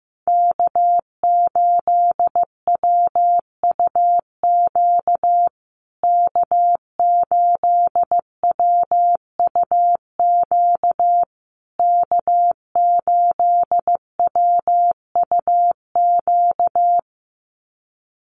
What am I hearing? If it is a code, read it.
Morse code, heard as a keyed tone, about 14 wpm, 700 Hz. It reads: K8WUQ K8WUQ K8WUQ
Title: USA Call Signs Repeated 3x 15wpm